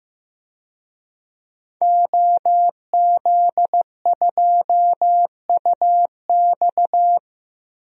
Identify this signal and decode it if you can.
Morse code, heard as a keyed tone, about 15 wpm, 700 Hz. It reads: OZ2UX
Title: DXCC-100 Call Signs - 5 Characters - Repeated 3x - 15wpm